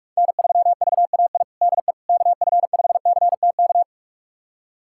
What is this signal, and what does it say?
Morse code, heard as a keyed tone, about 42 wpm, 700 Hz. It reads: N3VAI DE KF5CTK
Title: USA Callsign DE Callsign 45wpm